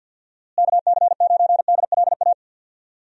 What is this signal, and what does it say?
Morse code, heard as a keyed tone, about 50 wpm, 700 Hz. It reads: KC9DLA
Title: USA Call Signs - 6 Characters - Repeated 3x - 50wpm